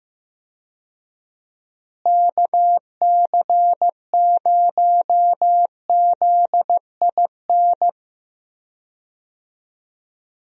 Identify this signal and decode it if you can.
Morse code, heard as a keyed tone, about 15 wpm, 700 Hz. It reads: KC0ZIN
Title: USA Call Signs - 6 Characters - Repeated 3x - 15wpm